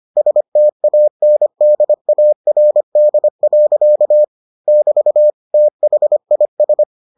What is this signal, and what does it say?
Morse code, heard as a keyed tone, about 25 wpm, 600 Hz. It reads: STANDARD. <BT> THIS